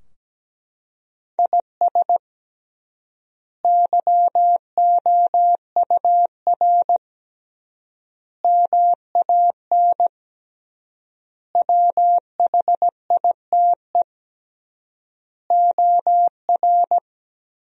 Morse code, heard as a keyed tone, about 17 wpm, 700 Hz. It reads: IS YOUR MAN WHITE OR
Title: 3x Word-Spacing - Sentences from Top 400 Words - 17wpm